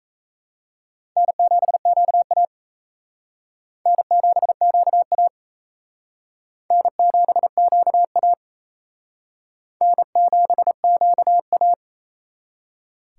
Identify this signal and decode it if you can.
Morse code, heard as a keyed tone, about 35 wpm, 700 Hz. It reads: N7QA N7QA N7QA N7QA
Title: Speed Racing - USA 1x2 Call Signs 28wpm